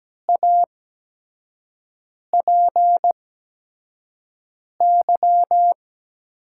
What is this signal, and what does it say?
Morse code, heard as a keyed tone, about 17 wpm, spaced as for 5 wpm, 700 Hz. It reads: APY
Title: USA Call Signs - 6 Characters - 8x Character Spacing - 17wpm